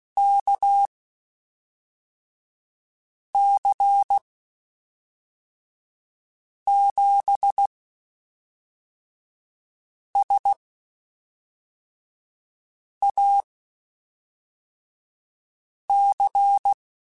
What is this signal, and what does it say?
Morse code, heard as a keyed tone, about 16 wpm, spaced as for 3 wpm, 800 Hz. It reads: KC7SAC
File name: (spaced) / QSOs Spaced-mp3s / kc7sac @5 sp